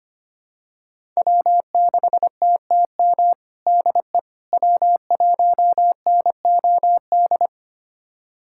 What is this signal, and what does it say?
Morse code, heard as a keyed tone, about 25 wpm, 700 Hz. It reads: W6TTM DE W1NOD
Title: USA Callsign DE Callsign - 5 Characters - 25wpm